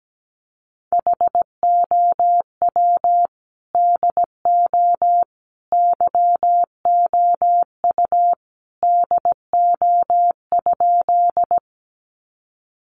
Morse code, heard as a keyed tone, about 17 wpm, 700 Hz. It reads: HOW DO YOU DO?
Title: Sentences from Top 100 Words 17wpm